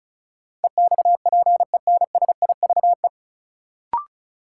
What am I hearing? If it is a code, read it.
Morse code, heard as a keyed tone, about 36 wpm, 700 Hz. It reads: EXPENSIVE
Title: Sentences from Top 1500 Words - New Word Drill Alt - 35wpm